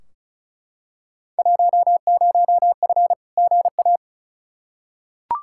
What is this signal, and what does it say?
Morse code, heard as a keyed tone, about 36 wpm, 700 Hz. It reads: 10F GA
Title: Field Day Exchanges Encore - 35wpm